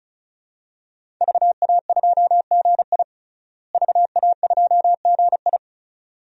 Morse code, heard as a keyed tone, about 36 wpm, 700 Hz. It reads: VA2GI VA2GI
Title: DXCC-100 Call Signs - 5 Characters - Repeated 2x - 35wpm